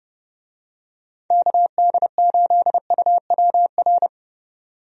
Morse code, heard as a keyed tone, about 30 wpm, 700 Hz. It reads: KD8UWR
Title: USA Call Signs - 6 Characters - Repeated 2x - 30wpm